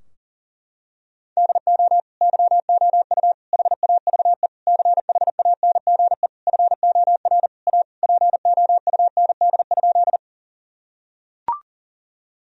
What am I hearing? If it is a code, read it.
Morse code, heard as a keyed tone, about 40 wpm, 700 Hz. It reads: DO YOU HAVE CHANGE FOR A POUND?
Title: Sentences from Top 1000 Words - Review Part 2 - 40wpm